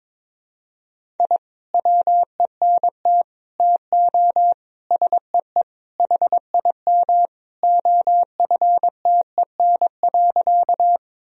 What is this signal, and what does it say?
Morse code, heard as a keyed tone, about 22 wpm, 700 Hz. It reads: I WENT TO SEE HIM OFTEN.
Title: Sentences from Top 1000 Words - Review Part 1 - 22wpm